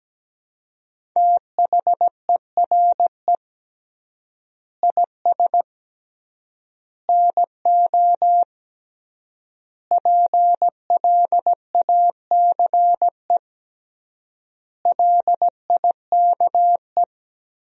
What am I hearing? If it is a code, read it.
Morse code, heard as a keyed tone, about 17 wpm, 700 Hz. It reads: THERE IS NO PLACE LIKE
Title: 3x Word-Spacing - Sentences from Top 200 Words - 17wpm